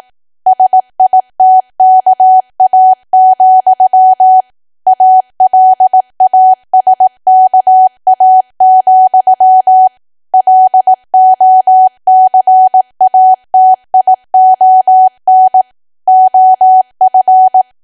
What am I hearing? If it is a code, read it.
Morse code, heard as a keyed tone, about 18 wpm, 750 Hz. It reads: SITKA, ALASKA, LOCATION OF
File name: cwa / W1AW-Level2 / 18WPM